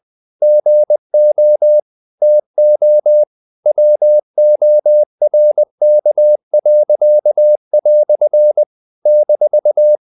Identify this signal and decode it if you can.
Morse code, heard as a keyed tone, about 20 wpm, 600 Hz. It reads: GO TO WORK." -
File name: CW / Practice / EnglishQuotations0320